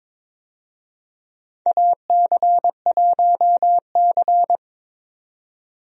Morse code, heard as a keyed tone, about 23 wpm, 700 Hz. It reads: AC1C
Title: USA Call Signs - 4 Characters - 22wpm